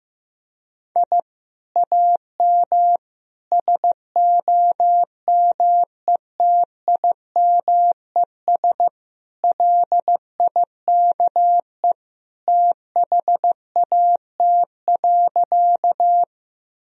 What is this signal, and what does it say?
Morse code, heard as a keyed tone, about 15 wpm, 700 Hz. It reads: I AM SOMETIMES LIKE THAT.
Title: Sentences from Top 1000 Words - Review Part 3 - 15wpm